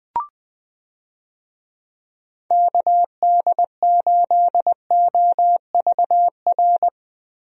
Morse code, heard as a keyed tone, about 21 wpm, 700 Hz. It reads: KD8OVR
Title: USA Call Signs - 6 Characters - Repeated 2x - 20wpm